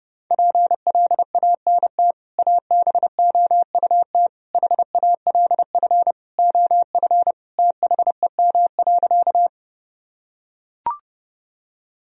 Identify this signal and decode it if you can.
Morse code, heard as a keyed tone, about 30 wpm, 700 Hz. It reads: PLANT ABOUT HALF OF THEM.
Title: Sentences from Top 1000 Words - Review Part 1 - 30wpm